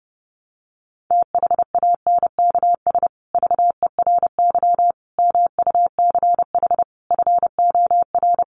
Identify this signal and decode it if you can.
Morse code, heard as a keyed tone, about 30 wpm, 700 Hz. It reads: THANKS VERY MUCH FOR
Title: Sets of 5 Words Encore2 30wpm